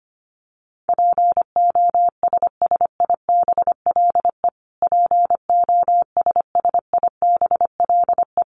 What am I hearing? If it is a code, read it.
Morse code, heard as a keyed tone, about 25 wpm, 700 Hz. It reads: POSSIBLE POSSIBLE
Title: Top 500 Words Repeated 25wpm